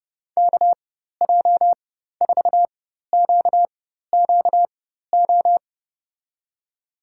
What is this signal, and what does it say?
Morse code, heard as a keyed tone, about 32 wpm, spaced as for 14 wpm, 700 Hz. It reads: KJ4QQO
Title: USA Call Signs - 6 Characters - 4x Character Spacing - 30wpm